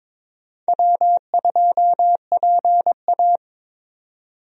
Morse code, heard as a keyed tone, about 22 wpm, 700 Hz. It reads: W2PA
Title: USA Call Signs - 4 Characters - 22wpm